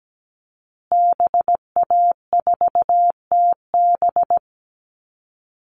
Morse code, heard as a keyed tone, about 17 wpm, 700 Hz. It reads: BA4TB
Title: Call Signs Repeated 2x - 17wpm